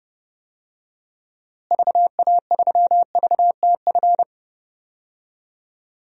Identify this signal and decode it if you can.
Morse code, heard as a keyed tone, about 30 wpm, 700 Hz. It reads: VA3VTF
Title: DXCC-100 Call Signs - 6 Characters - Repeated 2x - 30wpm